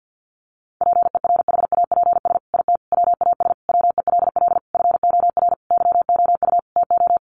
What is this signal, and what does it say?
Morse code, heard as a keyed tone, about 50 wpm, 700 Hz. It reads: PERHAPS IT WAS WELL FOR YOU TO
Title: Sentences from Top 1400 Words 50wpm